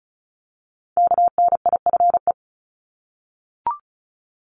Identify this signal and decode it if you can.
Morse code, heard as a keyed tone, about 36 wpm, 700 Hz. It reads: KNIFE
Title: Multi-letter Phonemes Repeated 3x 35wpm